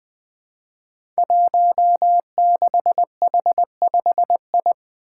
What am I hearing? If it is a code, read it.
Morse code, heard as a keyed tone, about 20 wpm, 700 Hz. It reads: 16H5I
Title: Five Challenging Letter-Number Combinations 20wpm